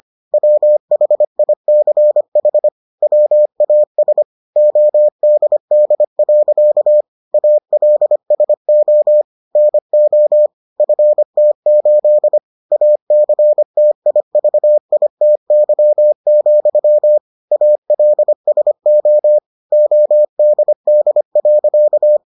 Morse code, heard as a keyed tone, about 25 wpm, 600 Hz. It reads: WHICH WAS ODD. ALSO NO FT8 ACTIVITY, ALSO ODD.